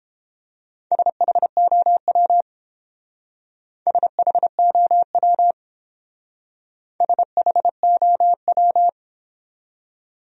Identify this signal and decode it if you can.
Morse code, heard as a keyed tone, about 30 wpm, 700 Hz. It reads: SHOW SHOW SHOW